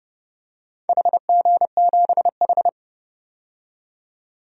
Morse code, heard as a keyed tone, about 30 wpm, 700 Hz. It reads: HG7H
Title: DXCC-100 Call Signs - 4 Characters - 30wpm